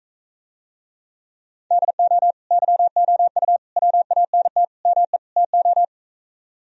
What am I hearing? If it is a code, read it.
Morse code, heard as a keyed tone, about 45 wpm, 700 Hz. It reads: DO YOU WANT ME TO